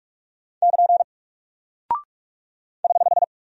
Morse code, heard as a keyed tone, about 48 wpm, 700 Hz. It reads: <KN> <HH>
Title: Single Character (Letter:Number:Punctuation:Prosign) Alt 45wpm